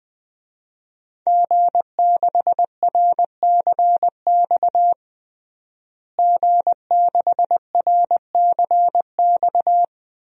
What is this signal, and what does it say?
Morse code, heard as a keyed tone, about 21 wpm, 700 Hz. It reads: G6RCX G6RCX
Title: DXCC-100 Call Signs - 5 Characters - Repeated 3x - 20wpm